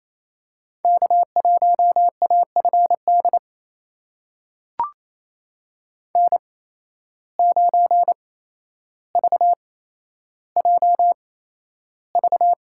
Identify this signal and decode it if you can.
Morse code, heard as a keyed tone, about 28 wpm, spaced as for 12 wpm, 700 Hz. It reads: K1AFD N9VJV
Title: USA Call Signs - 5 Characters - 8x Character Spacing - 28wpm